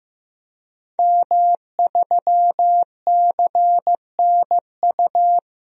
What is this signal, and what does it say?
Morse code, heard as a keyed tone, about 16 wpm, 700 Hz. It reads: M3CNU
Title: DXCC-100 Call Signs - 5 Characters - Repeated 3x - 15wpm